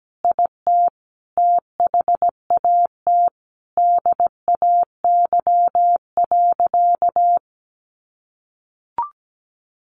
Morse code, heard as a keyed tone, about 17 wpm, 700 Hz. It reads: IT THAT DAY.
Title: Sentences from Top 1500 Words 17wpm